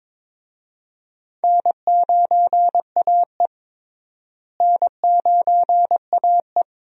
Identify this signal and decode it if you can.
Morse code, heard as a keyed tone, about 22 wpm, 700 Hz. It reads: N9AE N9AE
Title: USA Call Signs - 4 Characters - Repeated 2x - 22wpm